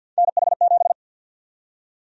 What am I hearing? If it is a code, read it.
Morse code, heard as a keyed tone, about 51 wpm, 700 Hz. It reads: NH7
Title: USA Call Sign Prefixes - 50wpm